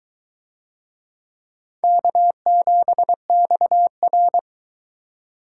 Morse code, heard as a keyed tone, about 22 wpm, 700 Hz. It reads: K7XR